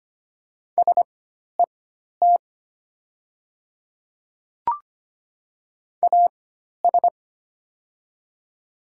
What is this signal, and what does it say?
Morse code, heard as a keyed tone, about 25 wpm, spaced as for 12 wpm, 700 Hz. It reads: SET AS
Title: 4x Character Spacing - Top 300 Words - 25wpm